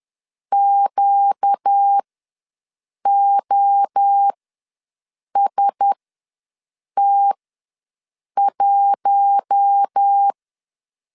Morse code, heard as a keyed tone, about 10 wpm, spaced as for 6 wpm, 800 Hz. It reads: QOST1
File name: LEZIONI / 6 PKX0 / 6 P ws 10wpm 3751